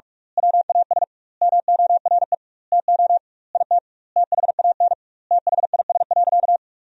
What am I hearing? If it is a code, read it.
Morse code, heard as a keyed tone, about 46 wpm, 700 Hz. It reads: WAS MORE TO IT THAN THIS.
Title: 2x Word-Spacing - Sentences from Top 100 Words - 45wpm